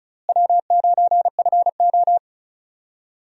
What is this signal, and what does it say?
Morse code, heard as a keyed tone, about 36 wpm, 700 Hz. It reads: W9FO
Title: USA Call Signs - 4 Characters - Repeated 3x - 35wpm